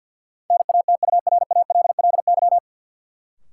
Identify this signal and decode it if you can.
Morse code, heard as a keyed tone, about 53 wpm, 700 Hz. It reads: NATURALLY
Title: Sentences from Top 1300 Words - New Word Drill - 50wpm